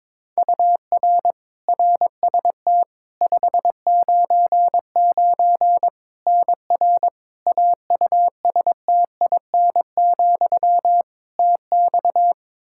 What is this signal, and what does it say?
Morse code, heard as a keyed tone, about 22 wpm, 700 Hz. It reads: UR RST 599 NR AUSTIN, TX
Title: Practice QSO - Exchange 1 - Beginning - 22wpm